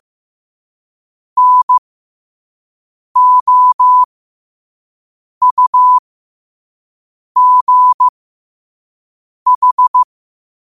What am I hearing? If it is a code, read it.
Morse code, heard as a keyed tone, about 15 wpm, spaced as for 5 wpm, 1000 Hz. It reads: NOUGH
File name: qotdfiles / 5wpm